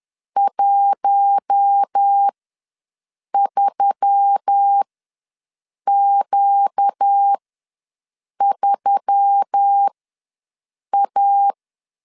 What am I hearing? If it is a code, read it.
Morse code, heard as a keyed tone, about 10 wpm, spaced as for 6 wpm, 800 Hz. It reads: 13Q3A
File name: LEZIONI / 5 LBVC / 5 L ws 10wpm 3636